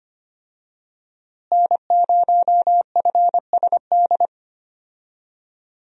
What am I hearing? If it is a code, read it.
Morse code, heard as a keyed tone, about 25 wpm, 700 Hz. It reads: N0FSD
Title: USA Call Signs - 5 Characters - Repeated 3x - 25wpm